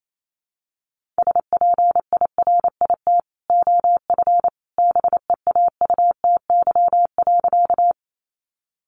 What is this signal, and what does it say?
Morse code, heard as a keyed tone, about 28 wpm, 700 Hz. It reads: SPIRIT OF BEAUTY.
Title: Sentences from Top 1400 Words 28wpm